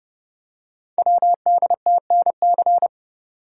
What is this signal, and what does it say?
Morse code, heard as a keyed tone, about 30 wpm, 700 Hz. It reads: WDTNC
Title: Code Groups - 5 Letters - 30wpm